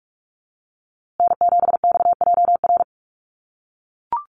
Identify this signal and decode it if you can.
Morse code, heard as a keyed tone, about 44 wpm, 700 Hz. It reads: N7XJR
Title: USA Call Signs Repeated 3x 45wpm